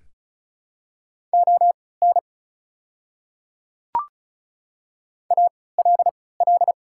Morse code, heard as a keyed tone, about 36 wpm, spaced as for 20 wpm, 700 Hz. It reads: ON ALL
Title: 3x Character Spacing - Top 100 Words - 35wpm